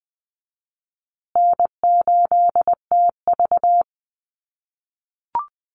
Morse code, heard as a keyed tone, about 20 wpm, 700 Hz. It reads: N8TV
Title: USA Call Signs Repeated 3x 20wpm